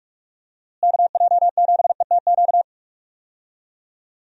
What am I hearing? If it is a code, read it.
Morse code, heard as a keyed tone, about 46 wpm, 700 Hz. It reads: KJ7ETQ
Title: USA Call Signs - 6 Characters - Repeated 2x - 45wpm